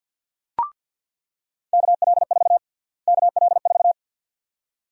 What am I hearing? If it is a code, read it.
Morse code, heard as a keyed tone, about 49 wpm, 700 Hz. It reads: KL4 KL4
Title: USA Call Sign Prefixes - Repeated 2x - 50wpm